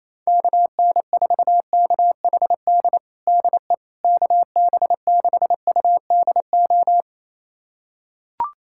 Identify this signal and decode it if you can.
Morse code, heard as a keyed tone, about 27 wpm, 700 Hz. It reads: KN4KHD DE KB6UDO
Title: USA Callsign DE Callsign 28wpm